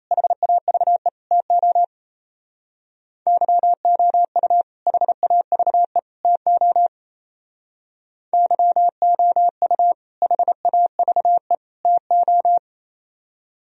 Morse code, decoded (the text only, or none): HAVE TO YOU HAVE TO YOU HAVE TO